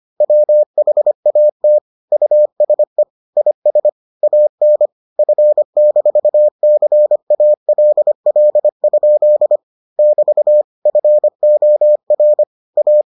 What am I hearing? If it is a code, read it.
Morse code, heard as a keyed tone, about 25 wpm, 600 Hz. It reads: WHAT USE IS AN F-CALL? <BT> FOR A